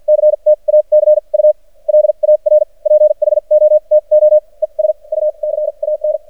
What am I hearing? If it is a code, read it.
Morse code, heard as a keyed tone, about 53 wpm, spaced as for 37 wpm, 610 Hz. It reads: CTACULAR PHOTO ES U CAN
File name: archived / CRSnet